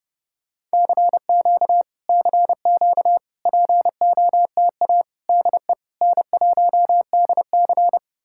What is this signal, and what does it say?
Morse code, heard as a keyed tone, about 33 wpm, 700 Hz. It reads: CQ CQ POTA DE N1DC